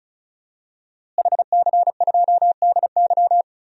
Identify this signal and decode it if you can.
Morse code, heard as a keyed tone, about 36 wpm, 700 Hz. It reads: HC2DY
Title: DXCC-100 Call Signs - 5 Characters - Repeated 3x - 35wpm